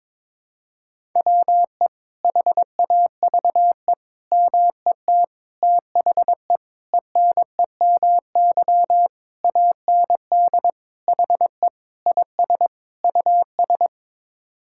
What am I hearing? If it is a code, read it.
Morse code, heard as a keyed tone, about 22 wpm, 700 Hz. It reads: WE HAVE MET THE ENEMY AND HE IS US
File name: Sets of 9 Words - 22wpm